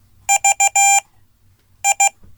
Morse code, heard as a keyed tone, about 15 wpm, spaced as for 8 wpm, 790 Hz. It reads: VI
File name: Code / 5min-b